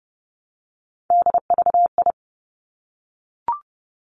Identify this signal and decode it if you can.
Morse code, heard as a keyed tone, about 30 wpm, 700 Hz. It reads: DVI
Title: Three Challenging Letter-Number Combinations 30wpm